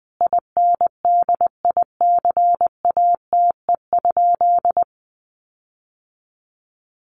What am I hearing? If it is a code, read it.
Morse code, heard as a keyed tone, about 20 wpm, 700 Hz. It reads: INDICATE?
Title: Sentences from Top 1300 Words 20wpm